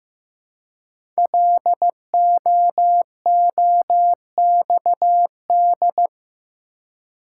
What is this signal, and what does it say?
Morse code, heard as a keyed tone, about 15 wpm, 700 Hz. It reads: LOOXD
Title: Code Groups - 5 Letter:Numbers - 15wpm